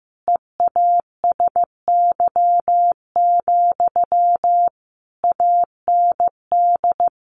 Morse code, heard as a keyed tone, about 15 wpm, 700 Hz. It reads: EASY, AND
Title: Sentences from Top 700 Words 15wpm